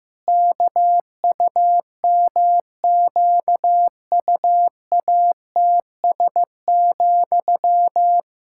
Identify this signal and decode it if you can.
Morse code, heard as a keyed tone, about 15 wpm, 700 Hz. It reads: KUMQUATS,
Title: Pangram Challenge v2 - 15wpm